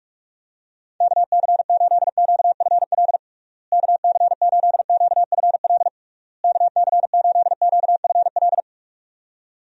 Morse code, heard as a keyed tone, about 46 wpm, 700 Hz. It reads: KC8QFL KC8QFL KC8QFL
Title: USA Call Signs - 6 Characters - Repeated 3x - 45wpm